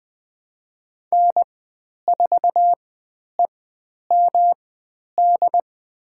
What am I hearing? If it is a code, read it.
Morse code, heard as a keyed tone, about 20 wpm, spaced as for 10 wpm, 700 Hz. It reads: N4EMD
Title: 109 Lesson 18g - Letter m - Call Signs 20wpm